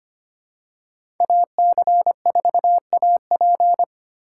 Morse code, heard as a keyed tone, about 25 wpm, 700 Hz. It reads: AC4AP
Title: USA Call Signs - 5 Characters - 4x Character Spacing - 25wpm